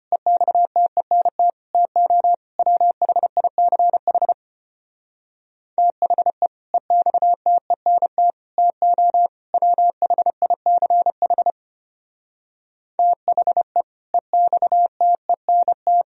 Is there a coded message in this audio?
Yes: Morse code from 0.1 to 16.0 s, heard as a keyed tone, about 32 wpm, 700 Hz. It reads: EXTENT TO WHICH THE EXTENT TO WHICH THE EXTENT